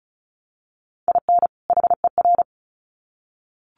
Morse code, heard as a keyed tone, about 34 wpm, 700 Hz. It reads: IN HER
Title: Sets of 2 Words 35wpm